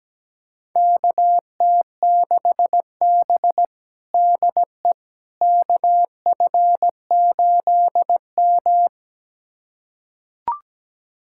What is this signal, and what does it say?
Morse code, heard as a keyed tone, about 17 wpm, 700 Hz. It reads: KT6B DE KF8M
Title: USA Callsign DE Callsign - 4 Characters - 17wpm